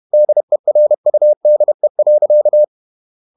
Morse code, heard as a keyed tone, about 31 wpm, 600 Hz. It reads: DERUDE.
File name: Ferie QRQ 31wpm